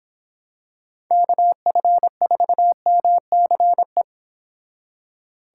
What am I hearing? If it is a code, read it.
Morse code, heard as a keyed tone, about 25 wpm, 700 Hz. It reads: KF4MCE